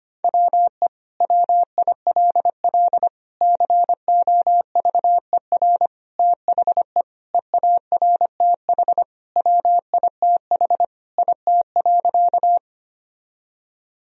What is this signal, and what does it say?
Morse code, heard as a keyed tone, about 25 wpm, 700 Hz. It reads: WE WILL COVER THE EARTH WITH IT.
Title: Sentences from Top 1000 Words - Review Part 1 - 25wpm